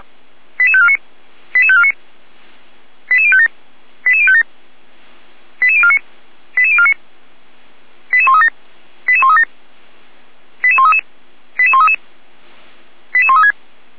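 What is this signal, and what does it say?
Five-tone selective call (ZVEI), heard as quick runs of five tones, tones of about 80 ms. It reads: 88549 88549 88057 88057 88049 88049 88137 88137 88133 88133 88136
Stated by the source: ZVEI five-tone selective call received by SDR.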